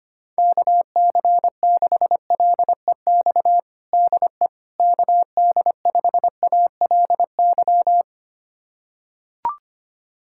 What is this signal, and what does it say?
Morse code, heard as a keyed tone, about 25 wpm, 700 Hz. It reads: KC6LEX DE KD5ALY
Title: USA Callsign DE Callsign 25wpm